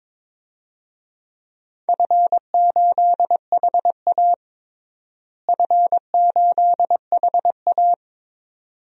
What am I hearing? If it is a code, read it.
Morse code, heard as a keyed tone, about 22 wpm, 700 Hz. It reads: F8HA F8HA
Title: DXCC-100 Call Signs - 4 Characters - Repeated 2x - 22wpm